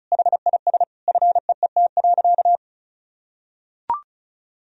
Morse code, heard as a keyed tone, about 36 wpm, 700 Hz. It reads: HIS FEET.
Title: Sentences from Top 1000 Words - Review Part 2 - 35wpm